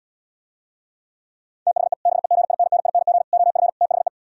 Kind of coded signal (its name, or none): Morse code